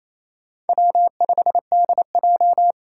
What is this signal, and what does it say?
Morse code, heard as a keyed tone, about 28 wpm, 700 Hz. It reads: W5DJ
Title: Four Challenging Letter-Number Combinations 28wpm